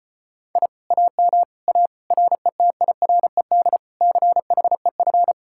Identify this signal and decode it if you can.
Morse code, heard as a keyed tone, about 33 wpm, 700 Hz. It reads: I AM A RETIRED CHEF